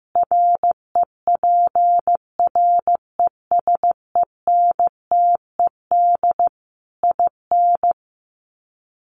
Morse code, heard as a keyed tone, about 15 wpm, 700 Hz. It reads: REPRESENTED IN
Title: Sets of 5 Words 15wpm